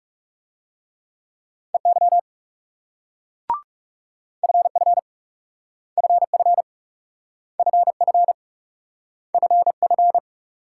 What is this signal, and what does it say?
Morse code, heard as a keyed tone, about 38 wpm, 700 Hz. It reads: EY FF FF FF FF